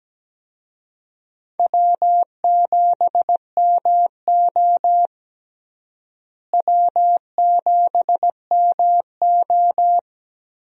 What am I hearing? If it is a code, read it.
Morse code, heard as a keyed tone, about 17 wpm, 700 Hz. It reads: W7MO W7MO
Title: USA Call Signs - 4 Characters - Repeated 3x - 17wpm